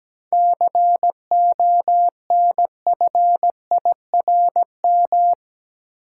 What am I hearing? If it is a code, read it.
Morse code, heard as a keyed tone, about 17 wpm, 700 Hz. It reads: CONFIRM
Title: Seven Letter Words - 17wpm